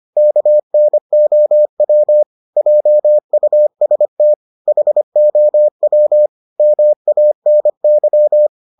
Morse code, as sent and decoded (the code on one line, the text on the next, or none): -.- -. --- .-- / .--- ..- ... - / .... --- .-- / -- .- -. -.--
KNOW JUST HOW MANY